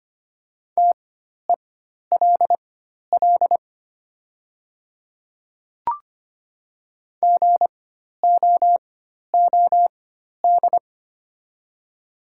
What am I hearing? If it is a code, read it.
Morse code, heard as a keyed tone, about 25 wpm, spaced as for 12 wpm, 700 Hz. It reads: TELL GOOD
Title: 4x Character Spacing - Top 200 Words - 25wpm